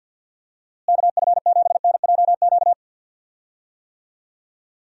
Morse code, heard as a keyed tone, about 49 wpm, 700 Hz. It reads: KF7NJQ
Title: USA Call Signs - 6 Characters - Repeated 3x - 50wpm